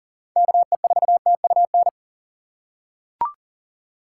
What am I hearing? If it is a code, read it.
Morse code, heard as a keyed tone, about 40 wpm, 700 Hz. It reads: KE4TUN
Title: USA Call Signs - 6 Characters - 6x Character Spacing - 40wpm